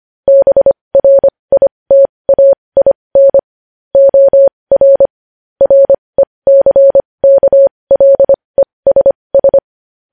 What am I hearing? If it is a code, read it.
Morse code, heard as a keyed tone, about 25 wpm, spaced as for 20 wpm, 550 Hz. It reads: BRITAIN OR RECKLESS
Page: News Headlines in Morse Code at 20 WPM